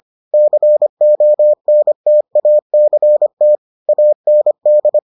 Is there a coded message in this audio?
Yes: Morse code from 0.3 to 5.0 s, heard as a keyed tone, about 25 wpm, 600 Hz. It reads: CONTACT AND